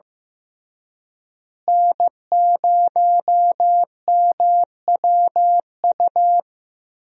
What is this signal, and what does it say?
Morse code, heard as a keyed tone, about 15 wpm, 700 Hz. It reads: N0MWU
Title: USA Call Signs - 5 Characters - Repeated 3x - 15wpm